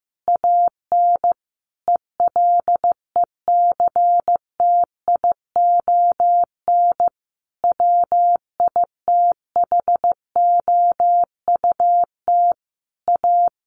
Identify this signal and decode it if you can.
Morse code, heard as a keyed tone, about 15 wpm, 700 Hz. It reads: AN ELECTION WITHOUT A
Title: Sentences from Top 1500 Words 15wpm